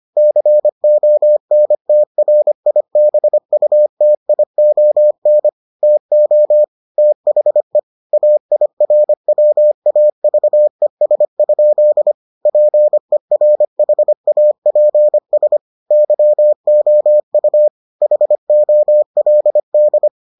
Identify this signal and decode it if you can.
Morse code, heard as a keyed tone, about 25 wpm, 600 Hz. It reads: CONTRIBUTION TO THE AIRWAVES? PERHAPS YOU HOLD